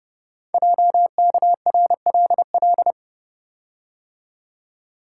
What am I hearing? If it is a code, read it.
Morse code, heard as a keyed tone, about 30 wpm, 690 Hz. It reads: JKRLL
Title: Code Groups - 5 Letters - 30wpm